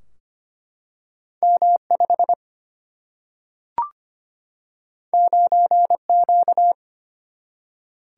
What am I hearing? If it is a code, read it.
Morse code, heard as a keyed tone, about 25 wpm, 700 Hz. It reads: M5 9Q
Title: Single-Letter Single-Number 25wpm